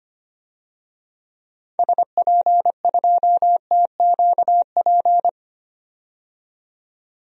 Morse code, heard as a keyed tone, about 25 wpm, 700 Hz. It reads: SP2TQP
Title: DXCC-100 Call Signs - 6 Characters - 25wpm